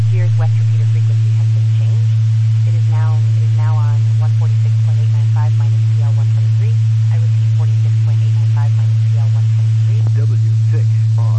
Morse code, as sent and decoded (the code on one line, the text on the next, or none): none